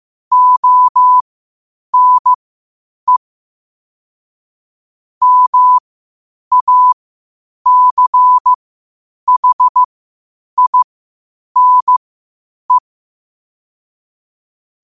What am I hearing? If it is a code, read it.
Morse code, heard as a keyed tone, about 15 wpm, spaced as for 8 wpm, 1000 Hz. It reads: ONE MACHINE
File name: qotdfiles / 8wpm